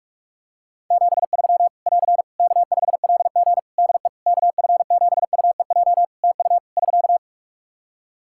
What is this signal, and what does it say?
Morse code, heard as a keyed tone, about 44 wpm, 700 Hz. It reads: 73 <AR> K5LG DE KF7UEJ TU <SK>
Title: K1USN SST QSO Practice 45wpm